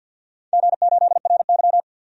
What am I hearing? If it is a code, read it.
Morse code, heard as a keyed tone, about 47 wpm, 700 Hz. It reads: G8RY
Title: DXCC-100 Call Signs - 4 Characters - Repeated 2x - 50wpm